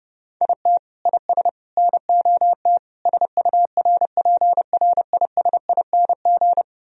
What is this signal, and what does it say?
Morse code, heard as a keyed tone, about 39 wpm, 700 Hz. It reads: IT IS NOT SURPRISING